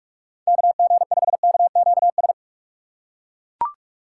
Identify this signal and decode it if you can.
Morse code, heard as a keyed tone, about 41 wpm, 700 Hz. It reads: KG5KQS
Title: USA Call Signs - 4x Character Spacing - 45wpm